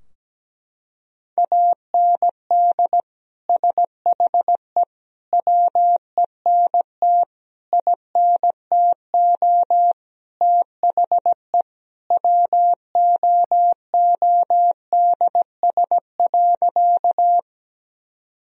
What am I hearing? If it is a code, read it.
Morse code, heard as a keyed tone, about 17 wpm, 700 Hz. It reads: AND SHE WENT INTO THE WOODS.
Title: Sentences from Top 1000 Words - Review Part 2 - 17wpm